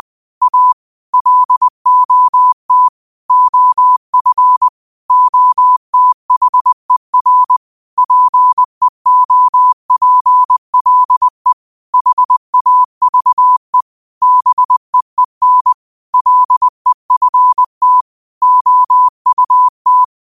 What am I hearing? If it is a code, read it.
Morse code, heard as a keyed tone, about 20 wpm, 1000 Hz. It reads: A LOT OF OTHER PEOPLE HAVE BEEN LEFT OUT